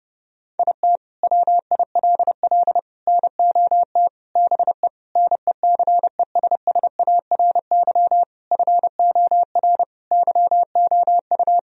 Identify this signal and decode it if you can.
Morse code, heard as a keyed tone, about 30 wpm, 700 Hz. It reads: IT WILL NOT BE NECESSARY FOR YOU
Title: Sentences from Top 1000 Words - Review Part 1 - 30wpm